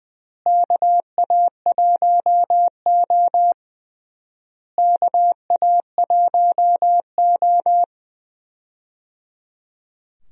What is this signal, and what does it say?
Morse code, heard as a keyed tone, about 21 wpm, 700 Hz. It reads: KA1O KA1O
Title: USA Call Signs - 4 Characters - Repeated 3x - 20wpm